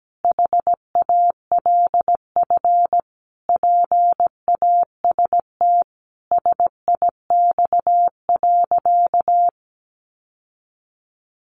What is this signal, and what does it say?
Morse code, heard as a keyed tone, about 17 wpm, 700 Hz. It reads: HALF PAST SIX.
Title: Sentences from Top 600 Words 17wpm